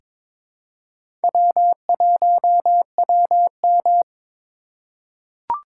Morse code, heard as a keyed tone, about 22 wpm, 700 Hz. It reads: W1WM
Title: USA Call Signs - 4 Characters - 4x Character Spacing - 22wpm